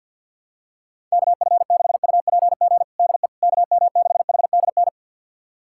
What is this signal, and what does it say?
Morse code, heard as a keyed tone, about 50 wpm, 700 Hz. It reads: KF6UPG DE KM6HDN
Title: USA Callsign DE Callsign - 6 Characters - 50wpm